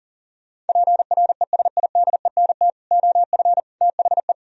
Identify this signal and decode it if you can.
Morse code, heard as a keyed tone, about 50 wpm, 700 Hz. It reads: PRESIDENT OF THE